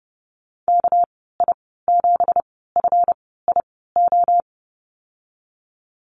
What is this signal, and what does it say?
Morse code, heard as a keyed tone, about 30 wpm, spaced as for 17 wpm, 700 Hz. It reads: KI7FIO
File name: Lesson 37g - Letter x - Call Signs 30wpm